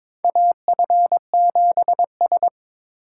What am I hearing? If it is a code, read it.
Morse code, heard as a keyed tone, about 22 wpm, 700 Hz. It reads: AF7S
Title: USA Call Signs - 4 Characters - 4x Character Spacing - 22wpm